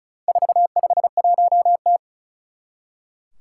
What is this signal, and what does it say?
Morse code, heard as a keyed tone, about 36 wpm, 690 Hz. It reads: 451T
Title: Four Challenging Letter-Number Combinations 35wpm